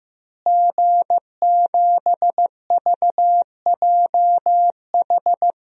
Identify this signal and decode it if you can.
Morse code, heard as a keyed tone, about 15 wpm, 700 Hz. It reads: G7VJH
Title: DXCC-100 Call Signs - 5 Characters - Repeated 3x - 15wpm